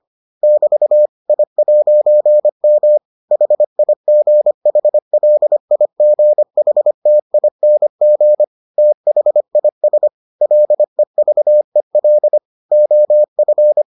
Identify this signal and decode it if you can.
Morse code, heard as a keyed tone, about 25 wpm, 600 Hz. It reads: <BT> I'M HIGHLIGHTING THIS LEVEL OF